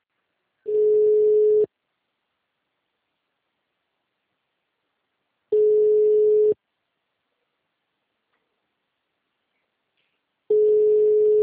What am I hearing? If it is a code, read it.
Ringback tone (Europe, 425 Hz).
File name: out- -2002- - - .21080